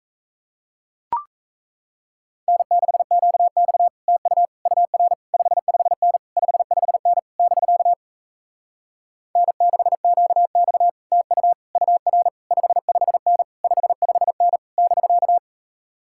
Morse code, heard as a keyed tone, about 37 wpm, 700 Hz. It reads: N6QX TU UR 55N 55N <BK> N6QX TU UR 55N 55N <BK>